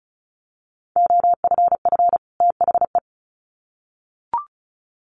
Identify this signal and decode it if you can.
Morse code, heard as a keyed tone, about 34 wpm, 700 Hz. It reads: OFF THE
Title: Sets of 2 Words 35wpm